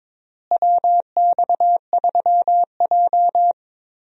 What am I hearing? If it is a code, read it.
Morse code, heard as a keyed tone, about 22 wpm, 700 Hz. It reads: WX3J
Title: USA Call Signs - 4 Characters - Repeated 2x - 22wpm